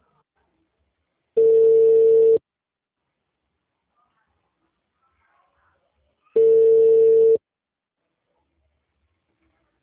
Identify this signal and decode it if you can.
Ringback tone (China, 450 Hz).
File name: out- -2019- - - .397